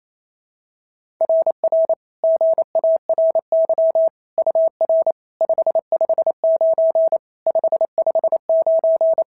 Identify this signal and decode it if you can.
Morse code, heard as a keyed tone, about 25 wpm, 650 Hz. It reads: RR GARY UR 559 559
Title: SOTA QSO Practice 28wpm